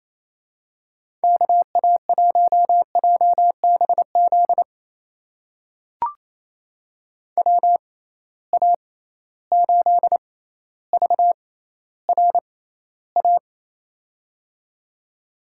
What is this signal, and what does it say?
Morse code, heard as a keyed tone, about 28 wpm, spaced as for 14 wpm, 700 Hz. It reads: KA1JBZ WA8VRA
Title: USA Call Signs - 6 Characters - 6x Character Spacing - 28wpm